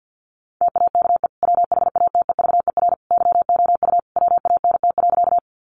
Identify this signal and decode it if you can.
Morse code, heard as a keyed tone, about 49 wpm, 700 Hz. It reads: TAKE WHATEVER YOU WANT.
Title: Sentences from Top 1100 Words 50wpm